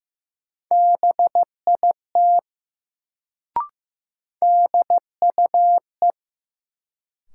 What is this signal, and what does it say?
Morse code, heard as a keyed tone, about 15 wpm, 700 Hz. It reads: BIT DUE
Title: Sentences from Top 900 Words - New Word Drill - 15wpm